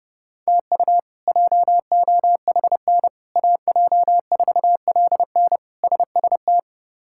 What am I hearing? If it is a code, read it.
Morse code, heard as a keyed tone, about 30 wpm, 700 Hz. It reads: TU JOHN AJ4LN SST
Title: K1USN SST QSO Practice - 30wpm